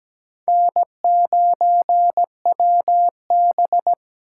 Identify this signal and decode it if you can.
Morse code, heard as a keyed tone, about 17 wpm, 700 Hz. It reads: N9WB
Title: USA Call Signs - 4 Characters - 3x Character Spacing - 17wpm